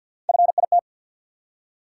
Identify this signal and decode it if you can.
Morse code, heard as a keyed tone, about 54 wpm, 700 Hz. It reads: FIT
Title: Three Letter Words - 50wpm